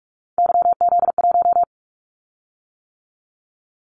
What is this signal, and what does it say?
Morse code, heard as a keyed tone, about 46 wpm, 700 Hz. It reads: YZ1
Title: Callsign Prefixes for Top 25 DXCC - 45wpm